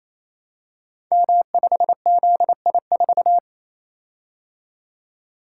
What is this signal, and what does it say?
Morse code, heard as a keyed tone, about 29 wpm, 700 Hz. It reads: M5ZI4
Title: Code Groups - 5 Letter:Numbers - 28wpm